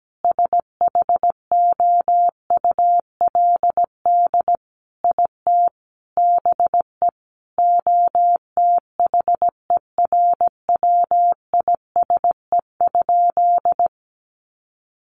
Morse code, heard as a keyed tone, about 17 wpm, 700 Hz. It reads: SHOULD IT BE OTHERWISE?
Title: Sentences from Top 1200 Words 17wpm